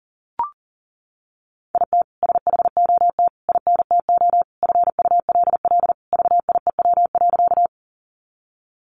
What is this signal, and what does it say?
Morse code, heard as a keyed tone, about 40 wpm, 700 Hz. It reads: IT SHOT INTO FULL VIEW.
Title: Sentences from Top 1100 Words 40wpm